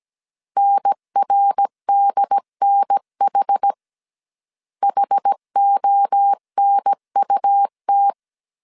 Morse code, heard as a keyed tone, about 17 wpm, 800 Hz. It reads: NRDNH HONUT
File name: CW / DET / 1 3 1500C S A 16wpm 2141